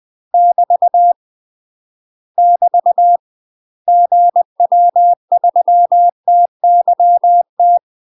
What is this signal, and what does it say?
Morse code, heard as a keyed tone, about 20 wpm, 700 Hz. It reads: <BT> <BT> GW3TYT
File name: UK MASTER QSOs TRAINING/20WPM